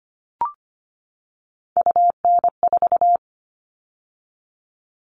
Morse code, heard as a keyed tone, about 25 wpm, 700 Hz. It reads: UN4
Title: All Cut Numbers - 3-Digit 25wpm